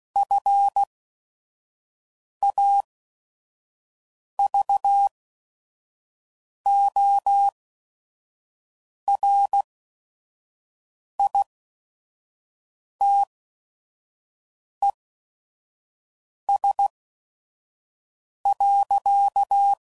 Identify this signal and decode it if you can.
Morse code, heard as a keyed tone, about 16 wpm, spaced as for 5 wpm, 800 Hz. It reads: FAVORITES.
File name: CW Code Groups @5WPM / QSOs / WD0FBT 2